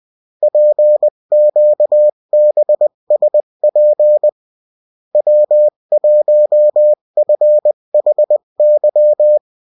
Morse code, heard as a keyed tone, about 20 wpm, 600 Hz. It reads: PQBSP W1FHY
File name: morse mp3 / revision9-vitesse 18 mots